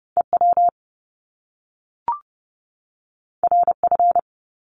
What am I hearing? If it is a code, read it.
Morse code, heard as a keyed tone, about 30 wpm, 700 Hz. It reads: EW RF
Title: Top Two Letter Combinations in Words 30wpm